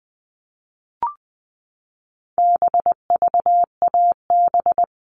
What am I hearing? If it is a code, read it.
Morse code, heard as a keyed tone, about 20 wpm, 700 Hz. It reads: BVAB
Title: All Cut Numbers - 4-Digit 20wpm